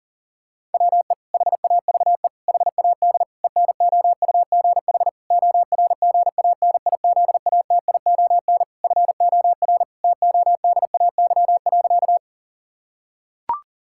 Morse code, heard as a keyed tone, about 40 wpm, 700 Hz. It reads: WE HAVE HAD ENOUGH ORGANIZATION FOR TODAY.
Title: Sentences from Top 1000 Words - Review Part 5 - 40wpm